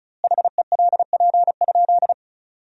Morse code, heard as a keyed tone, about 34 wpm, 700 Hz. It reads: HELP?
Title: 2x Word-Spacing - Sentences from Top 200 Words - 35wpm